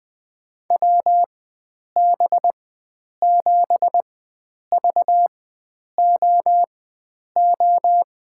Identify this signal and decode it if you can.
Morse code, heard as a keyed tone, about 21 wpm, spaced as for 9 wpm, 700 Hz. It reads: WB7VOO
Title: USA Call Signs - 6 Characters - 4x Character Spacing - 20wpm